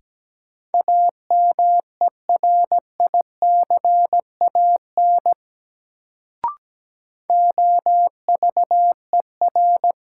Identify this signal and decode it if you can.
Morse code, heard as a keyed tone, about 17 wpm, 700 Hz. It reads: AMERICAN OVER
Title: Sets of 2 Words - New Word Drill - 17wpm